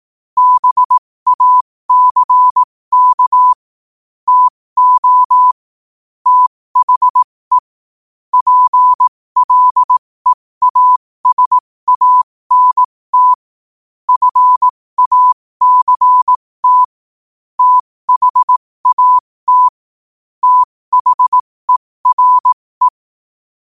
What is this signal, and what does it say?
Morse code, heard as a keyed tone, about 18 wpm, spaced as for 15 wpm, 1000 Hz. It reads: BACK TO THE PLEASANT FACT THAT THERE
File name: qotdfiles / 15wpm